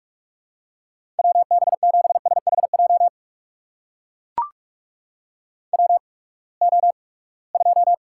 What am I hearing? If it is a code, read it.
Morse code, heard as a keyed tone, about 44 wpm, spaced as for 25 wpm, 700 Hz. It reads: WB7SHJ WO2
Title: USA Call Signs - 8x Character Spacing - 45wpm